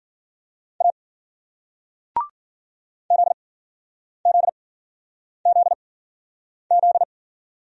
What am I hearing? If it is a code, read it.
Morse code, heard as a keyed tone, about 56 wpm, spaced as for 9 wpm, 700 Hz. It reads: A ZZZZ